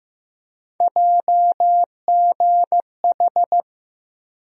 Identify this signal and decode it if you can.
Morse code, heard as a keyed tone, about 15 wpm, 700 Hz. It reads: JGH
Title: USA Call Sign Postfixes - 3 Letters 15wpm